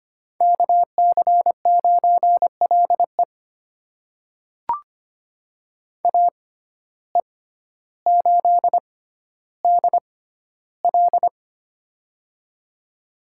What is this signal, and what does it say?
Morse code, heard as a keyed tone, about 25 wpm, spaced as for 13 wpm, 700 Hz. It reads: KC9LE AE8DL
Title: USA Call Signs - 5 Characters - 6x Character Spacing - 25wpm